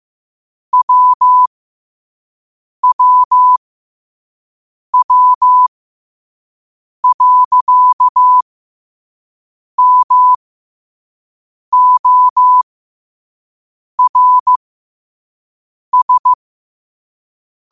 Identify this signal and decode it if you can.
Morse code, heard as a keyed tone, about 15 wpm, spaced as for 5 wpm, 1000 Hz. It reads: WWW.MORS
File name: qotdfiles / 5wpm